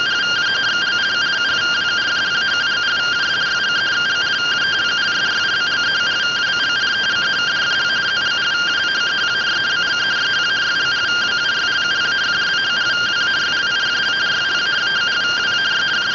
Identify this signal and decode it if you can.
Two-tone teleprinter signal, about 170 Hz shift.